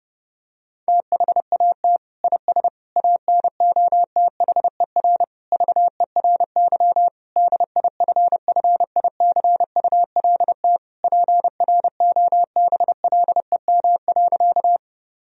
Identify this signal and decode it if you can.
Morse code, heard as a keyed tone, about 30 wpm, 700 Hz. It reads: THAT IS ANOTHER VERY DIFFICULT PROBLEM.
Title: Sentences from Top 1000 Words - Review Part 2 - 30wpm